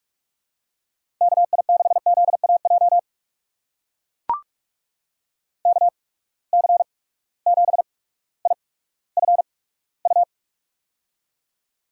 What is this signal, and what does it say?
Morse code, heard as a keyed tone, about 44 wpm, spaced as for 19 wpm, 700 Hz. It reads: KI6ZAJ KC7IFU
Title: USA Call Signs - 6 Characters - 8x Character Spacing - 45wpm